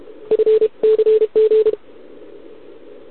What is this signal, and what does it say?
Morse code, heard as a keyed tone, about 31 wpm, 440 Hz. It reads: FCG
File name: ARCHIV / OKDX2020 / OK1RIb